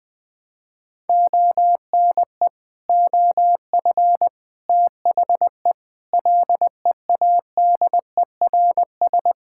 Morse code, heard as a keyed tone, about 20 wpm, 700 Hz. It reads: ONE OF THE LEADERS
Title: Sets of 6 Words - 20wpm